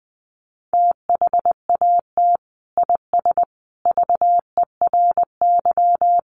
Morse code, heard as a keyed tone, about 20 wpm, 700 Hz. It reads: THAT IS VERY
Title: Sentences from Top 1100 Words 20wpm